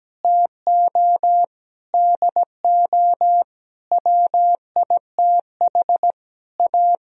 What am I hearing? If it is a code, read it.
Morse code, heard as a keyed tone, about 17 wpm, 700 Hz. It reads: TO DO WITH A
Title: Sentences from Top 1000 Words - Review Part 2 - 17wpm